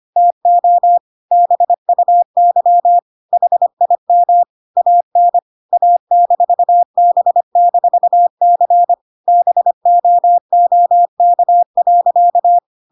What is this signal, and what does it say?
Morse code, heard as a keyed tone, about 25 wpm, 700 Hz. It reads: TO BUY HIM AN A-B-C BOOK.